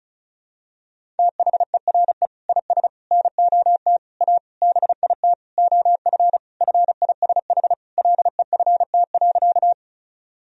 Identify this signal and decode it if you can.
Morse code, heard as a keyed tone, about 36 wpm, 700 Hz. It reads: THERE IS NOT A BIT OF FISH LEFT.
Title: Sentences from Top 1000 Words - Review Part 1 - 35wpm